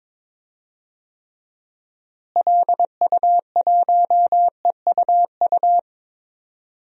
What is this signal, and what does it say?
Morse code, heard as a keyed tone, about 22 wpm, 700 Hz. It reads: LU1EUU
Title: DXCC-100 Call Signs - 6 Characters - Repeated 2x - 22wpm